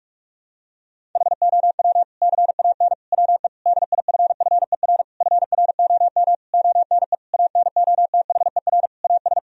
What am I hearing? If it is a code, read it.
Morse code, heard as a keyed tone, about 46 wpm, 700 Hz. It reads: HOW CAN WE DIFFER FROM ONE ANOTHER AS
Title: Sentences from Top 1000 Words - Review Part 1 - 45wpm